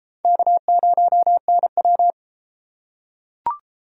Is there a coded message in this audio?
Yes: Morse code from 0.2 to 2.1 s, heard as a keyed tone, about 33 wpm, 700 Hz. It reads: K0NW